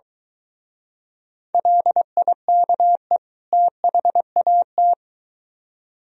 Morse code, heard as a keyed tone, about 22 wpm, 700 Hz. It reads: LIKE THAT